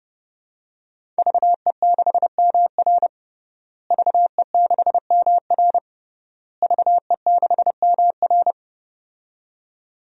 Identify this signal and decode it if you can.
Morse code, heard as a keyed tone, about 32 wpm, 700 Hz. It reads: VE6MR VE6MR VE6MR
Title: DXCC-100 Call Signs - 5 Characters - Repeated 3x - 30wpm